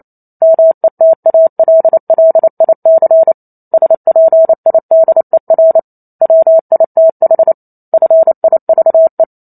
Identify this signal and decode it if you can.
Morse code, heard as a keyed tone, about 28 wpm, 650 Hz. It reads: METALLIC SPIDER WITH FIVE